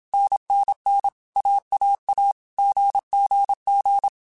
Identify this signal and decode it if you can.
Morse code, heard as a keyed tone, about 27 wpm, 800 Hz. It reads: NNN AAA GGG
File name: Basic Sets / Basic Letter Groups Collective7 / AWUNDG@25